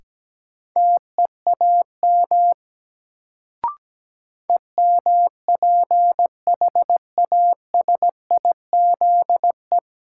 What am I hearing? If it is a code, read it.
Morse code, heard as a keyed tone, about 17 wpm, 700 Hz. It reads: TEAM EMPHASIZE
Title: Sentences from Top 1500 Words - New Word Drill - 17wpm